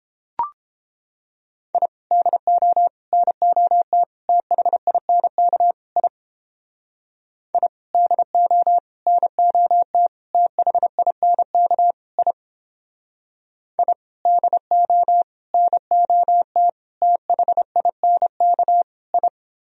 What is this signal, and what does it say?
Morse code, heard as a keyed tone, about 31 wpm, 700 Hz. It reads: I DO NOT THINK I I DO NOT THINK I I DO NOT THINK I